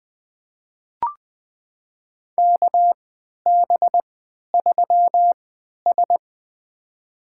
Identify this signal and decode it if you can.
Morse code, heard as a keyed tone, about 20 wpm, spaced as for 11 wpm, 700 Hz. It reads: KB3S
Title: USA Call Signs - 3x Character Spacing - 20wpm